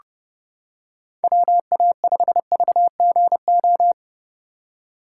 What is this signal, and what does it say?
Morse code, heard as a keyed tone, about 31 wpm, 700 Hz. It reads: WA5VGO